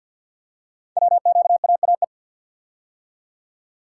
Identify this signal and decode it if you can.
Morse code, heard as a keyed tone, about 52 wpm, 700 Hz. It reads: WQAAE
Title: Code Groups - 5 Letter:Numbers - 50wpm